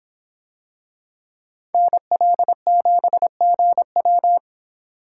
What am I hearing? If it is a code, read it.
Morse code, heard as a keyed tone, about 26 wpm, 700 Hz. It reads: NL7GW